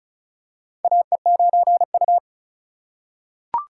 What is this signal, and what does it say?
Morse code, heard as a keyed tone, about 36 wpm, 700 Hz. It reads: AE9U
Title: USA Call Signs - 4 Characters - 3x Character Spacing - 35wpm